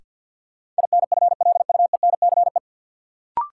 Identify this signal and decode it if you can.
Morse code, heard as a keyed tone, about 53 wpm, 700 Hz. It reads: INFLUENCE
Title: Sentences from Top 1100 Words - New Word Drill Alt - 50wpm